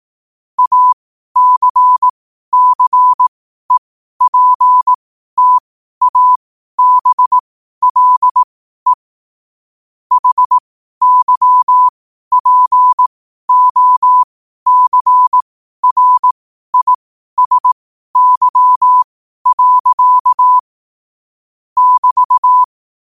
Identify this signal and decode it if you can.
Morse code, heard as a keyed tone, about 18 wpm, spaced as for 12 wpm, 1000 Hz. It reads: ACCEPTABLE HYPOCRISY. <BT>
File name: qotdfiles / 12wpm